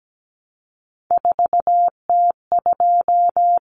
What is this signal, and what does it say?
Morse code, heard as a keyed tone, about 17 wpm, 700 Hz. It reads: 4T2
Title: Common Cut Numbers - 3-Digit 17wpm